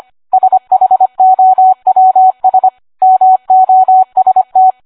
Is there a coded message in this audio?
Yes: Morse code from 0.3 to 4.7 s, heard as a keyed tone, about 25 wpm, 750 Hz, with radio noise. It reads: SHOWS MOST